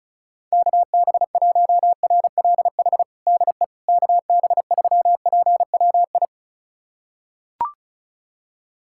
Morse code, heard as a keyed tone, about 34 wpm, 700 Hz. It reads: KB1RLH DE KB3PWI
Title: USA Callsign DE Callsign 35wpm